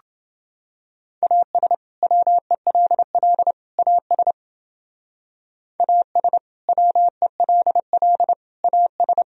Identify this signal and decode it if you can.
Morse code, heard as a keyed tone, about 30 wpm, 700 Hz. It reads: AS WELL AS AS WELL AS